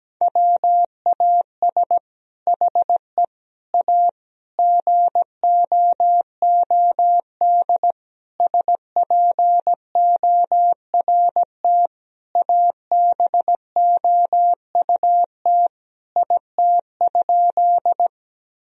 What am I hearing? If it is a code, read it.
Morse code, heard as a keyed tone, about 17 wpm, 700 Hz. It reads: WAS HE A GOOD SPORT ABOUT IT?
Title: Sentences from Top 1000 Words - Review Part 4 - 17wpm